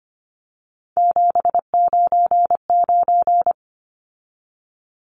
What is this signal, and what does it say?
Morse code, heard as a keyed tone, about 25 wpm, 700 Hz. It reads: 799
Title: Numbers - 3-Digit 25wpm